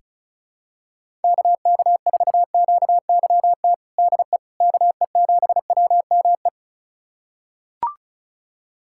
Morse code, heard as a keyed tone, about 34 wpm, 700 Hz. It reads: KK4QYT DE KE7WME
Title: USA Callsign DE Callsign - 6 Characters - 35wpm